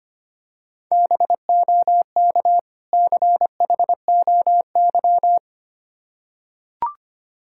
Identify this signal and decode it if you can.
Morse code, heard as a keyed tone, about 25 wpm, 700 Hz. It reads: BOK CHOY
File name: practice / 25wpm / Difficult Words with Hints - 25wpm